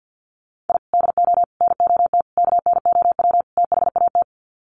Morse code, heard as a keyed tone, about 50 wpm, 700 Hz. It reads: I DO NOT KNOW THAT
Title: Sets of 5 Words 50wpm